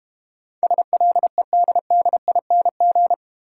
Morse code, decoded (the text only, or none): SLEDDING